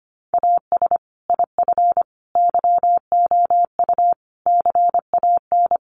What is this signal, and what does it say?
Morse code, heard as a keyed tone, about 25 wpm, 700 Hz. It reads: AS IF YOU CAN
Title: Sentences from Top 1300 Words 25wpm